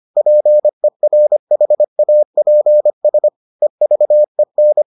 Morse code, heard as a keyed tone, about 25 wpm, 600 Hz. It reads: PERHAPS EVEN